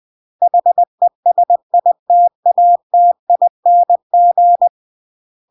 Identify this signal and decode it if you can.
Morse code, heard as a keyed tone, about 20 wpm, 700 Hz. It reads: HESITATING
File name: CW / jodorowsky / 82 commandments 20-20